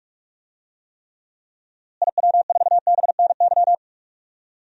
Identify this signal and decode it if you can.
Morse code, heard as a keyed tone, about 46 wpm, 700 Hz. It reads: IW4BNY
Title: DXCC-100 Call Signs - 6 Characters - Repeated 3x - 45wpm